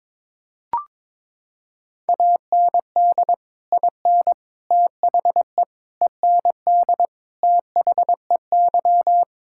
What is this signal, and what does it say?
Morse code, heard as a keyed tone, about 22 wpm, 700 Hz. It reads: AND IN THE END THEY
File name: practice / 22wpm / Sentences with 2-Word Phrases - 22wpm